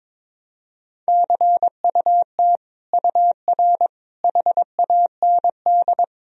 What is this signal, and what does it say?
Morse code, heard as a keyed tone, about 22 wpm, 700 Hz. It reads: CUT UR HAND
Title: Sentences from Top 300 Words and Abbreviations 22wpm